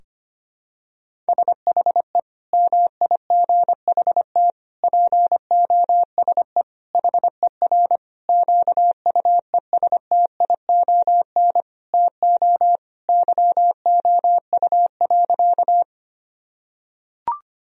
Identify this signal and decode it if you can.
Morse code, heard as a keyed tone, about 25 wpm, 700 Hz. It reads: SHE MIGHT POSE HER QUESTION TO YOU.
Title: Sentences from Top 1000 Words - Review Part 2 - 25wpm